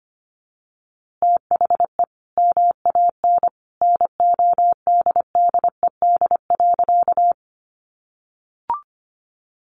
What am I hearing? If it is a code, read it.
Morse code, heard as a keyed tone, about 25 wpm, 700 Hz. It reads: THE MAN NODDED.
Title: A Sound Of Thunder 25wpm